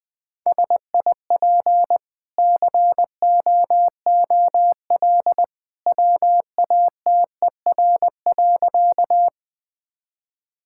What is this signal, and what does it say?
Morse code, heard as a keyed tone, about 21 wpm, 700 Hz. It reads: SIP COOL WATER.
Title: Easy Sentences with Hints - 20wpm